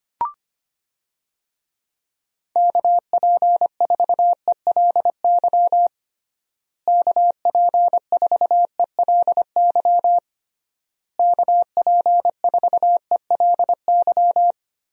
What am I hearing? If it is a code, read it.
Morse code, heard as a keyed tone, about 25 wpm, 700 Hz. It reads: KP4ELY KP4ELY KP4ELY
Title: USA Call Signs - 6 Characters - Repeated 3x - 25wpm